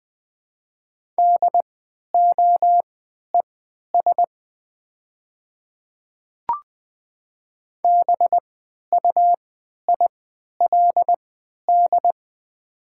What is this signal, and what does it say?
Morse code, heard as a keyed tone, about 20 wpm, spaced as for 11 wpm, 700 Hz. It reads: DOES BUILD
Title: 3x Character Spacing - Top 300 Words - 20wpm